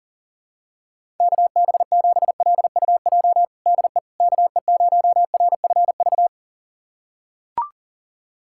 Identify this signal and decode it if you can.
Morse code, heard as a keyed tone, about 37 wpm, 700 Hz. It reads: KB7LUJ DE KE0RFV
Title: USA Callsign DE Callsign - 6 Characters - 40wpm